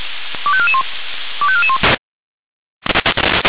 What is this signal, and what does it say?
Five-tone selective call (ZVEI), heard as quick runs of five tones, tones of about 70 ms. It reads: 25441 25441